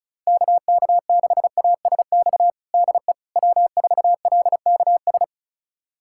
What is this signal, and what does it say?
Morse code, heard as a keyed tone, about 36 wpm, 700 Hz. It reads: KK6ASX DE W4LKS
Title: USA Callsign DE Callsign 35wpm